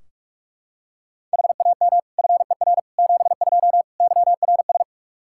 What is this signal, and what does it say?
Morse code, heard as a keyed tone, about 46 wpm, 700 Hz. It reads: HAM FER 72 YRS
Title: Practice QSO - Exchange 3 - More Info - 45wpm